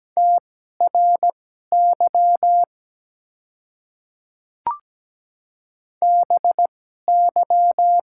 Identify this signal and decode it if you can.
Morse code, heard as a keyed tone, about 17 wpm, spaced as for 12 wpm, 700 Hz. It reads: TRY BY
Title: 2x Character Spacing - Top 200 Words - 17wpm